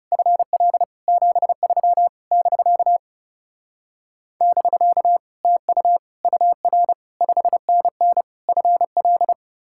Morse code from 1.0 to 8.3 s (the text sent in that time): 73 <BK> <BK> TU UR 5NN